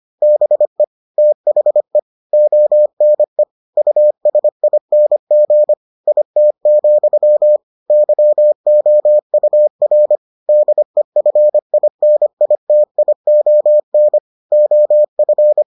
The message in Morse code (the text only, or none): BE THE ONE USING IT, YOUR DEFINITION OF